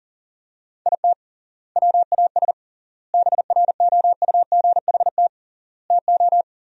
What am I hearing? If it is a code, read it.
Morse code, heard as a keyed tone, about 40 wpm, 700 Hz. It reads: IT WAS BROUGHT TO
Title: 3x Word-Spacing - Sentences from Top 500 Words - 40wpm